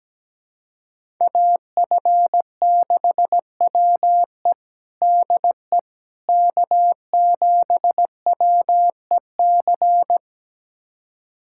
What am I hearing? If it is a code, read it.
Morse code, heard as a keyed tone, about 17 wpm, 700 Hz. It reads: AF6WE DE K7WEC
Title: USA Callsign DE Callsign - 5 Characters - 17wpm